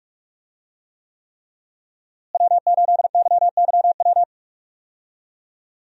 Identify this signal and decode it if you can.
Morse code, heard as a keyed tone, about 44 wpm, 700 Hz. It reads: W8YYW
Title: USA Call Signs - 5 Characters - Repeated 3x - 45wpm